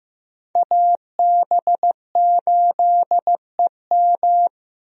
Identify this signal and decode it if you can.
Morse code, heard as a keyed tone, about 15 wpm, 700 Hz. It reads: AB8EM
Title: USA Call Signs - 5 Characters - Repeated 2x - 15wpm